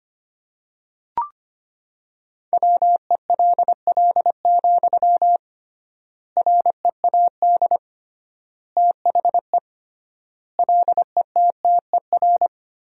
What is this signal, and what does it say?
Morse code, heard as a keyed tone, about 25 wpm, 700 Hz. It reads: WELL, READ THE LETTER
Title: 3x Word-Spacing - Sentences from Top 300 Words - 25wpm